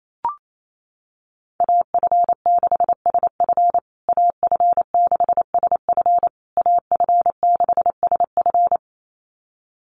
Morse code, heard as a keyed tone, about 28 wpm, 700 Hz. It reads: AF6SF AF6SF AF6SF
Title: USA Call Signs Repeated 3x 28wpm